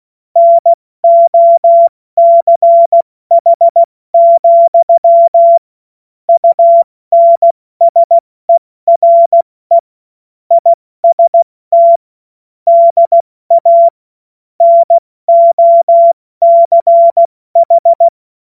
Morse code, heard as a keyed tone, about 16 wpm, 680 Hz. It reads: NOCH, UNSERE IST DA NOCH
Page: cw-witze